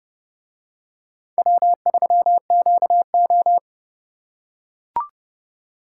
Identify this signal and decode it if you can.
Morse code, heard as a keyed tone, about 29 wpm, 700 Hz. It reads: W3QO